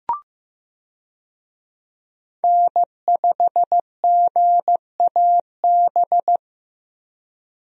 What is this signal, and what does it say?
Morse code, heard as a keyed tone, about 15 wpm, 700 Hz. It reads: N5GAB
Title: USA Call Signs - 5 Characters - Repeated 2x - 15wpm